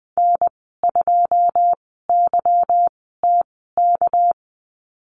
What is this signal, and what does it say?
Morse code, heard as a keyed tone, about 20 wpm, 700 Hz. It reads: N 2 Y T K
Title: 207 Lesson 33g - Letter j - Call Signs 20wpm